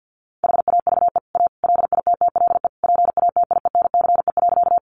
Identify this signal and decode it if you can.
Morse code, heard as a keyed tone, about 50 wpm, 700 Hz. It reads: HAVE A LITTLE PATIENCE.
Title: Sentences from Top 1300 Words 50wpm